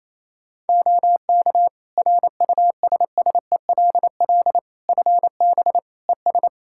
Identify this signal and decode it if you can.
Morse code, heard as a keyed tone, about 28 wpm, 700 Hz. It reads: OK RUSSELL FB ES
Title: Practice QSO - Exchange 2 - Info - 28wpm